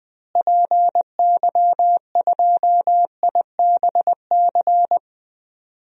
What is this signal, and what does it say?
Morse code, heard as a keyed tone, about 20 wpm, 700 Hz. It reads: PY2IBC
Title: DXCC-100 Call Signs - 6 Characters - Repeated 3x - 20wpm